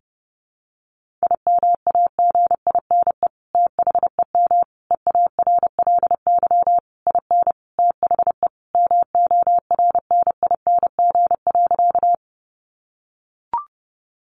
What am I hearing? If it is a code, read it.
Morse code, heard as a keyed tone, about 28 wpm, 700 Hz. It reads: IMAGINE THEM EARLY IN THE MORNING.
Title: Sentences from Top 1400 Words 30wpm